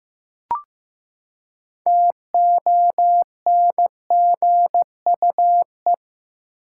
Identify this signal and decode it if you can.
Morse code, heard as a keyed tone, about 15 wpm, 700 Hz. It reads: TONGUE
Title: Six Letter Words - 15wpm